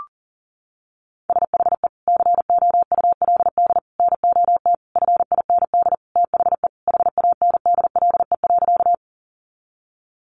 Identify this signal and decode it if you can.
Morse code, heard as a keyed tone, about 40 wpm, 700 Hz. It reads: SHE COULD NOT FIND THE HANDLE.
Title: Sentences from Top 1100 Words 40wpm